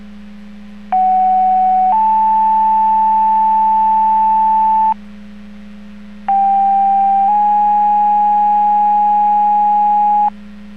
Two-tone page, heard as a one-second tone, then a three-second tone, about 750 then 880 Hz, about 790 then 830 Hz. It is heard from 0.9 to 10.3 s.